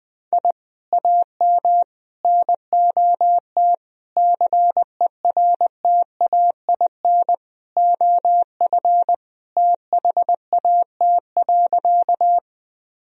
Morse code, heard as a keyed tone, about 21 wpm, 700 Hz. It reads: I AM NOT CERTAIN OF THAT.
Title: Sentences from Top 1000 Words - Review Part 2 - 20wpm